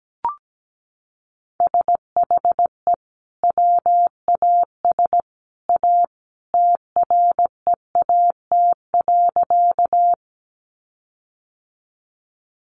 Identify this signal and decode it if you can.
Morse code, heard as a keyed tone, about 17 wpm, 700 Hz. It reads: SHE WAS A TREAT.
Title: Sentences from Top 1300 Words 17wpm